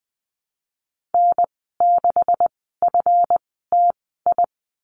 Morse code, heard as a keyed tone, about 20 wpm, spaced as for 15 wpm, 700 Hz. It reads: N6FTI
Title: 173 Lesson 28g - Number 9 - Call Signs 20wpm